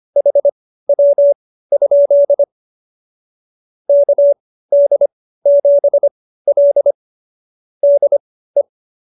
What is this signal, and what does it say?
Morse code, heard as a keyed tone, about 25 wpm, spaced as for 15 wpm, 570 Hz. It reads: HW? KD7L DE
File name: qso222 15